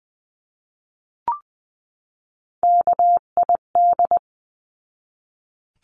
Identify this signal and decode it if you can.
Morse code, heard as a keyed tone, about 20 wpm, 700 Hz. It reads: KID
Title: 259 Lesson 42d - Words 18-20wpm